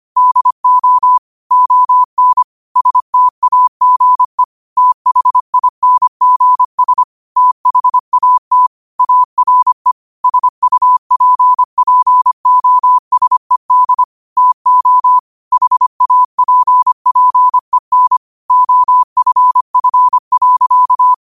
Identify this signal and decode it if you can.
Morse code, heard as a keyed tone, about 25 wpm, 1000 Hz. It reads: DO ON STAGE THINGS THAT ARE SUPPOSED TO HAPPEN OFF.
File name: qotdfiles / 25wpm